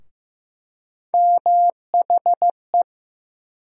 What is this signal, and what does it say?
Morse code, heard as a keyed tone, about 15 wpm, 700 Hz. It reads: MHE
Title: USA Call Sign Postfixes - 3 Letters 15wpm